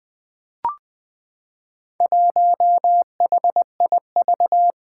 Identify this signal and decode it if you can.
Morse code, heard as a keyed tone, about 20 wpm, 700 Hz. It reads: 1HIV
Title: Four Challenging Letter-Number Combinations 20wpm